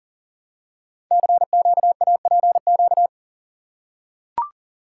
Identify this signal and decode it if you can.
Morse code, heard as a keyed tone, about 40 wpm, 700 Hz. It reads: CQAPQ
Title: Code Groups - 5 Letters - 40wpm